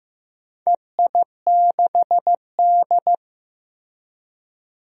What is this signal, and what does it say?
Morse code, heard as a keyed tone, about 15 wpm, 700 Hz. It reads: EI6D
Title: DXCC-100 Call Signs - 4 Characters - Repeated 2x - 15wpm